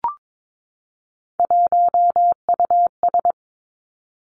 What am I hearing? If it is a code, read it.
Morse code, heard as a keyed tone, about 22 wpm, 700 Hz. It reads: 1US
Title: Three Challenging Letter-Number Combinations 22wpm